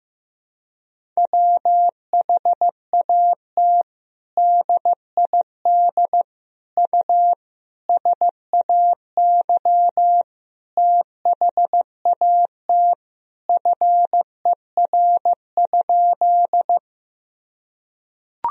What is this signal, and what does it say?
Morse code, heard as a keyed tone, about 14 wpm, 700 Hz. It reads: WHAT DID U SAY THAT FER?
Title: Sentences from Top 200 Words and Abbreviations 15wpm